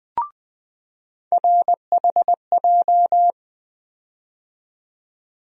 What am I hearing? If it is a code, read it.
Morse code, heard as a keyed tone, about 20 wpm, 700 Hz. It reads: RHJ
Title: USA Call Sign Postfixes - 3 Letters 20wpm